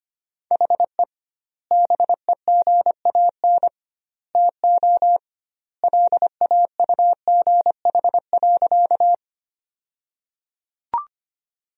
Morse code, heard as a keyed tone, about 25 wpm, 700 Hz. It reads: HE BEGAN TO LAUGH.
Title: 2x Word-Spacing - Sentences from Top 500 Words - 25wpm